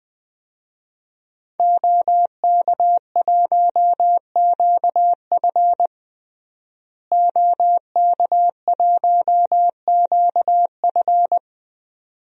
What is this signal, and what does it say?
Morse code, heard as a keyed tone, about 21 wpm, 700 Hz. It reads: OK1QF OK1QF
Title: DXCC-100 Call Signs - 5 Characters - Repeated 2x - 20wpm